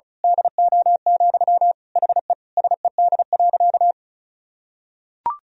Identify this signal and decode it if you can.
Morse code, heard as a keyed tone, about 36 wpm, 700 Hz. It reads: DO, HE SED.
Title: Sentences from Top 100 Words and Abbreviations 35wpm